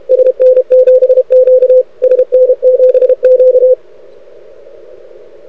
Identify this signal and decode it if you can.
Morse code, heard as a keyed tone, about 30 wpm, 500 Hz. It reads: SN7Q SN7Q
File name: rec contest / arrl cw 2018 / test 32